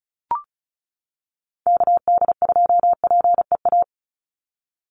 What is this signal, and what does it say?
Morse code, heard as a keyed tone, about 34 wpm, 700 Hz. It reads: KD2PEA
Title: USA Call Signs 35wpm